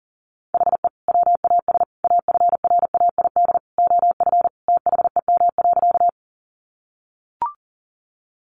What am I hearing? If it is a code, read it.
Morse code, heard as a keyed tone, about 40 wpm, 700 Hz. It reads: HE WAS AFRAID OF THEM.
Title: Sentences from Top 1500 Words 40wpm